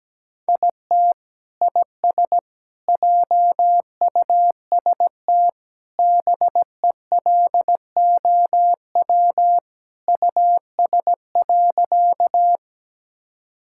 Morse code, heard as a keyed tone, about 17 wpm, 700 Hz. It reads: IT IS JUST BELOW US.
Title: Sentences from Top 1000 Words - Review Part 5 - 17wpm